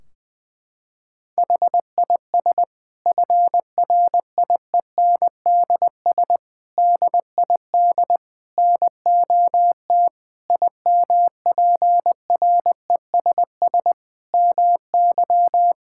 Morse code, heard as a keyed tone, about 20 wpm, 700 Hz. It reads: HIS FRIENDS DID NOT IMPRESS MY
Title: Sentences from Top 1600 Words 20wpm